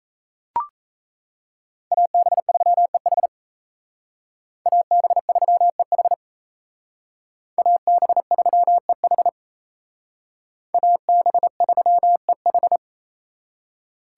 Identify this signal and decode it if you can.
Morse code, heard as a keyed tone, about 34 wpm, 700 Hz. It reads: AB3EH AB3EH AB3EH AB3EH
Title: Speed Racing - USA 2x2 Call Signs 28wpm